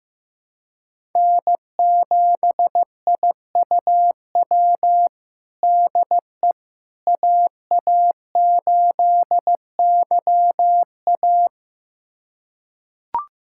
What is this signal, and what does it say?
Morse code, heard as a keyed tone, about 15 wpm, 700 Hz. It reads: N7IUW DE AA8YA
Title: USA Callsign DE Callsign - 5 Characters - 15wpm